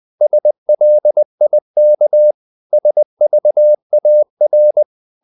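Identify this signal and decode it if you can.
Morse code, heard as a keyed tone, about 20 wpm, 600 Hz. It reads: SLIK SVAR
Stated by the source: Morse code practice of short Danish words at 20 wpm, from CW med Gnister.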